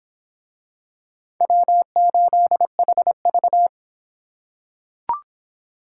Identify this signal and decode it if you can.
Morse code, heard as a keyed tone, about 23 wpm, 700 Hz. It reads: W8HV